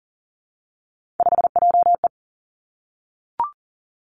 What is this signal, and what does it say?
Morse code, heard as a keyed tone, about 42 wpm, 700 Hz. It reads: 5JE
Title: Three Challenging Letter-Number Combinations 40wpm